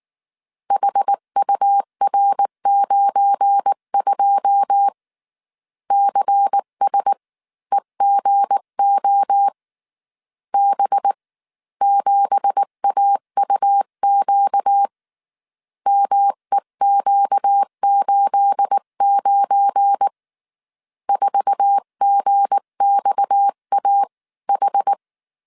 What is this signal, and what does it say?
Morse code, heard as a keyed tone, about 19 wpm, spaced as for 16 wpm, 800 Hz. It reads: HUR92 CSEGO B7AUQ MEQ89 4GXAH
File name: CW / DET / 1 9 1500C S B 18wpm 2551